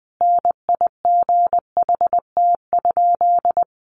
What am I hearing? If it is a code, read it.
Morse code, heard as a keyed tone, about 20 wpm, 700 Hz. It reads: NIGHT?
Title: Sentences from Top 600 Words 20wpm